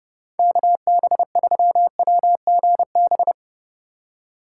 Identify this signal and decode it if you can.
Morse code, heard as a keyed tone, about 31 wpm, 700 Hz. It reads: KB3WGB
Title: USA Call Signs - 8x Character Spacing - 30wpm